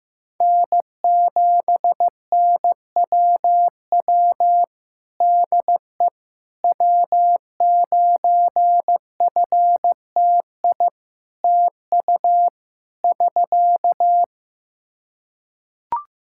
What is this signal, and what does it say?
Morse code, heard as a keyed tone, about 15 wpm, 700 Hz. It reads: N7NWW DE W9FTI TU <SK>
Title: Practice QSO - Exchange 4 - Ending - 15wpm